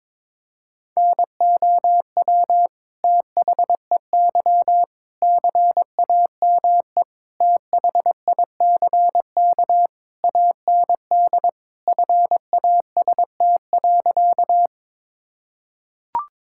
Morse code, heard as a keyed tone, about 22 wpm, 700 Hz. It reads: NOW THEY CAME THICK AND FAST.
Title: Sentences from Top 1600 Words 22wpm